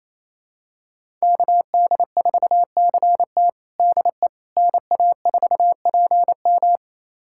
Morse code, heard as a keyed tone, about 28 wpm, 700 Hz. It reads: KD4CT DE NA4PM
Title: USA Callsign DE Callsign - 5 Characters - 28wpm